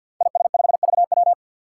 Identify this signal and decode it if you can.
Morse code, heard as a keyed tone, about 50 wpm, 700 Hz. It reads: IS5VW
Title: Five Challenging Letter-Number Combinations 50wpm